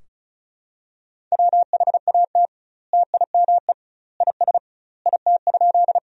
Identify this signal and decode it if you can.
Morse code, heard as a keyed tone, about 36 wpm, 700 Hz. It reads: WHAT TIME IS IT?
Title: 2x Word-Spacing - Sentences from Top 100 Words - 35wpm